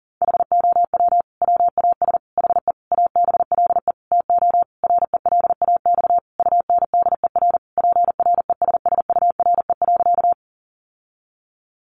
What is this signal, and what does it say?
Morse code, heard as a keyed tone, about 40 wpm, 700 Hz. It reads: HOW WAS HE ABLE TO RELAX UNDER PRESSURE.
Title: Sentences from Top 1400 Words 40wpm